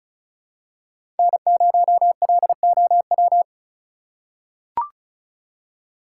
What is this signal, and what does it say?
Morse code, heard as a keyed tone, about 36 wpm, 690 Hz. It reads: N0LOW
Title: USA Call Signs - 5 Characters - 3x Character Spacing - 35wpm